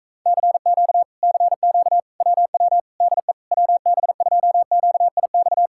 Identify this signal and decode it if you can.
Morse code, heard as a keyed tone, about 38 wpm, 700 Hz. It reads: CQ CQ WW DE WB2QIX